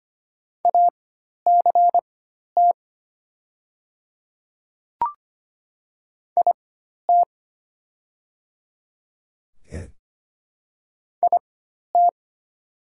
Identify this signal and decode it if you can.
Morse code, heard as a keyed tone, about 25 wpm, spaced as for 12 wpm, 700 Hz. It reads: ACT IT IT
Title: 4x Character Spacing - Top 300 Words - 25wpm